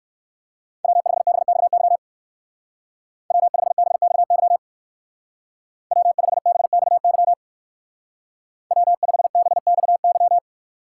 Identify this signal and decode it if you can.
Morse code, heard as a keyed tone, about 59 wpm, 700 Hz. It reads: W5BXY W5BXY W5BXY W5BXY